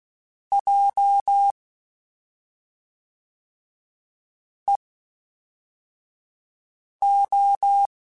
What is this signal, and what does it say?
Morse code, heard as a keyed tone, about 16 wpm, 800 Hz. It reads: J E O
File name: CW Comparisons / CW Comparisons-mp3s / CW Comparisons Broken Characters @5